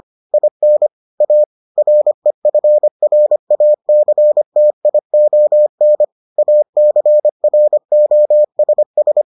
Morse code, heard as a keyed tone, about 25 wpm, 600 Hz. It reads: IN A REFRACTION ACROSS